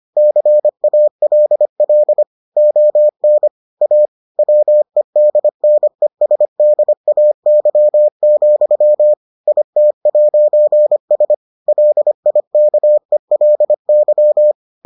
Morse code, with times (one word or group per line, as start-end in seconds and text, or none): 0.2-2.2 CALL
2.6-3.5 ON
3.8-4.1 A
4.4-9.1 WEDNESDAY,
9.5-11.3 IT'S
11.7-14.5 LIKELY